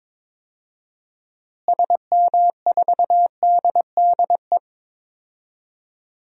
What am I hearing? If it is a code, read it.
Morse code, heard as a keyed tone, about 22 wpm, 700 Hz. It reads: SM4DDE
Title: DXCC-100 Call Signs - 6 Characters - Repeated 2x - 22wpm